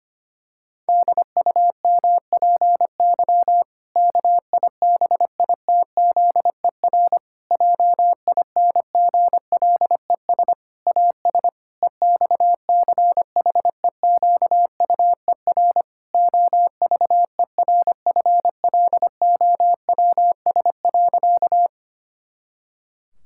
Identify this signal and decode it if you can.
Morse code, heard as a keyed tone, about 25 wpm, 700 Hz. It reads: DUMPY KIBITZER JINGLES AS EXCHEQUER OVERFLOWS.
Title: Pangram Challenge v2 - 25wpm